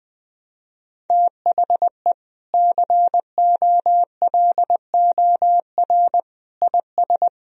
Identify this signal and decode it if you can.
Morse code, heard as a keyed tone, about 20 wpm, 700 Hz. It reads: THE COLOR IS
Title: Sentences from Top 1000 Words - Review Part 2 - 20wpm